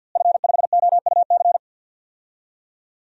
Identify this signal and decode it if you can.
Morse code, heard as a keyed tone, about 47 wpm, 700 Hz. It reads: F5OUC
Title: DXCC-100 Call Signs - 5 Characters - Repeated 2x - 50wpm